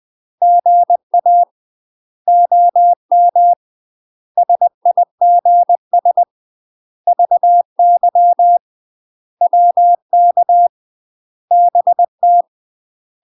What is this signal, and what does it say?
Morse code, heard as a keyed tone, about 20 wpm, 700 Hz. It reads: GA OM SIGS VY WK BT
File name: UK MASTER QSOs TRAINING / 20WPM / UK MASTER QSOs 0058